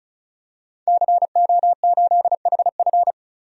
Morse code, heard as a keyed tone, about 36 wpm, 700 Hz. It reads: CO8HF
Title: DXCC-100 Call Signs - 5 Characters - 35wpm